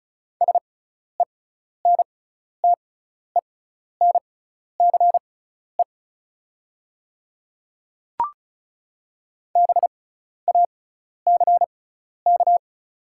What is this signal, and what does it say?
Morse code, heard as a keyed tone, about 36 wpm, spaced as for 12 wpm, 700 Hz. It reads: SENTENCE BACK
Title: 6x Character Spacing - Top 200 Words - 35wpm